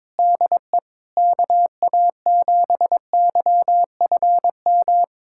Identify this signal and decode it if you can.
Morse code, heard as a keyed tone, about 22 wpm, 700 Hz. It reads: DE KA7YFM
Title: USA Callsign DE Callsign - 6 Characters - 22wpm